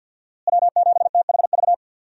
Morse code, heard as a keyed tone, about 47 wpm, 700 Hz. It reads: W7THV
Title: USA Call Signs - 5 Characters - Repeated 2x - 50wpm